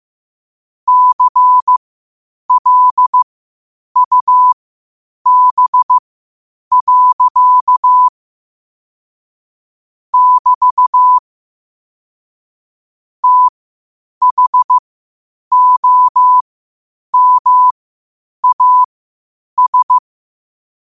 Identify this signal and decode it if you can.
Morse code, heard as a keyed tone, about 15 wpm, spaced as for 8 wpm, 1000 Hz. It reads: CLUB. <BT> THOMAS
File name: qotdfiles / 8wpm